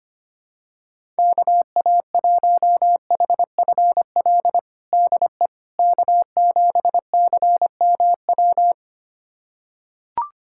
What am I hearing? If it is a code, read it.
Morse code, heard as a keyed tone, about 25 wpm, 700 Hz. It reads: KA1HFL DE K7CMW
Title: USA Callsign DE Callsign 25wpm